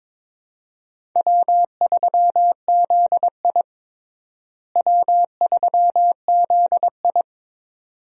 Morse code, heard as a keyed tone, about 22 wpm, 700 Hz. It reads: W3ZI W3ZI
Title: USA Call Signs - 4 Characters - Repeated 2x - 22wpm